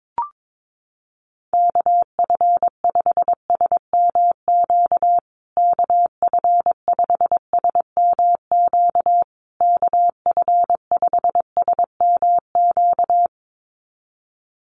Morse code, heard as a keyed tone, about 22 wpm, 700 Hz. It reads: KF5SMQ KF5SMQ KF5SMQ
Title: USA Call Signs Repeated 3x 22wpm